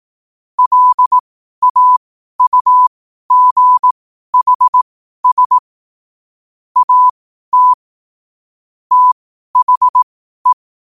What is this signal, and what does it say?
Morse code, heard as a keyed tone, about 18 wpm, spaced as for 12 wpm, 1000 Hz. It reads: LAUGHS AT THE
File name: qotdfiles / 12wpm